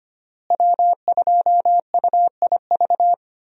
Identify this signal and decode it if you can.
Morse code, heard as a keyed tone, about 25 wpm, 700 Hz. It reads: W2UIV
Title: USA Call Signs - 5 Characters - Repeated 3x - 25wpm